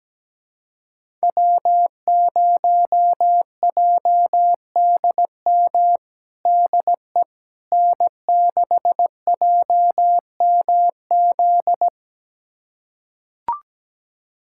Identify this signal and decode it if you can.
Morse code, heard as a keyed tone, about 17 wpm, 700 Hz. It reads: W0JDM DE N6JMZ
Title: USA Callsign DE Callsign 17wpm